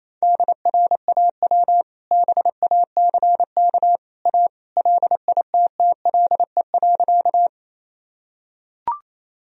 Morse code, heard as a keyed tone, about 28 wpm, 700 Hz. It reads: DRAW BACK A LITTLE.
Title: Sentences from Top 1000 Words - Review Part 1 - 28wpm